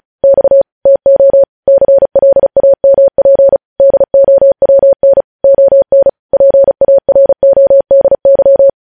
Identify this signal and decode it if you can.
Morse code, heard as a keyed tone, about 36 wpm, 550 Hz. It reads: X TO CLAMP DOWN ON PARODY
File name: podcast / News headlines-1x-35x35